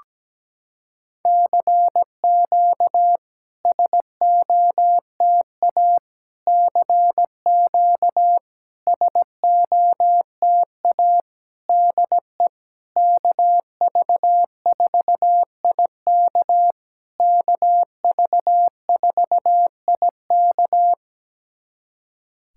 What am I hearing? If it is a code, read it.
Morse code, heard as a keyed tone, about 17 wpm, 700 Hz. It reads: CQ SOTA CQ SOTA DE KV4IK KV4IK
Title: SOTA QSO Practice 17wpm